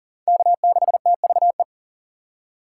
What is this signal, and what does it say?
Morse code, heard as a keyed tone, about 40 wpm, 700 Hz. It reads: K6TVE
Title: USA Call Signs - 5 Characters - 4x Character Spacing - 40wpm